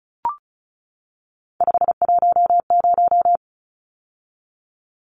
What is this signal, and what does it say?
Morse code, heard as a keyed tone, about 34 wpm, 700 Hz. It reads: 510
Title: Numbers - 3-Digit 35wpm